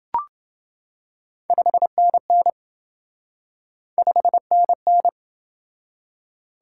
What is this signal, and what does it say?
Morse code, heard as a keyed tone, about 28 wpm, 700 Hz. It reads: 5NN 5NN